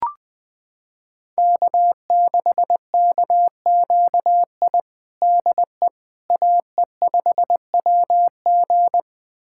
Morse code, heard as a keyed tone, about 20 wpm, 700 Hz. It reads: K6KQI DE AE5WG
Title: USA Callsign DE Callsign - 5 Characters - 20wpm